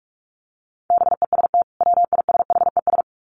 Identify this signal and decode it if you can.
Morse code, heard as a keyed tone, about 44 wpm, 700 Hz. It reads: BEST WISHES
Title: Common QSO Phrases 45wpm